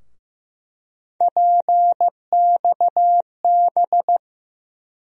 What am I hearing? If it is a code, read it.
Morse code, heard as a keyed tone, about 15 wpm, 700 Hz. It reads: PXB
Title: USA Call Sign Postfixes - 3 Letters 15wpm